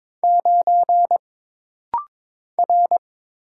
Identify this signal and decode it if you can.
Morse code, heard as a keyed tone, about 22 wpm, 700 Hz. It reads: 9 R
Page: This rapid-fire timing is geared to push your ICR (Instant Character Recognition) skills.